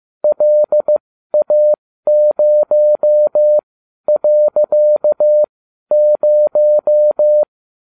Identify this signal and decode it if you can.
Morse code, heard as a keyed tone, about 15 wpm, spaced as for 12 wpm, 600 Hz. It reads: LA0.0